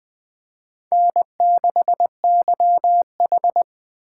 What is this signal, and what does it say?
Morse code, heard as a keyed tone, about 20 wpm, 700 Hz. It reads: N6YH
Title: Speed Racing - USA 1x2 Call Signs 17wpm